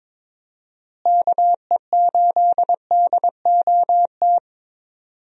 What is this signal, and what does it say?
Morse code, heard as a keyed tone, about 22 wpm, 700 Hz. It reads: KE8DOT
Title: USA Call Signs - 6 Characters - Repeated 3x - 22wpm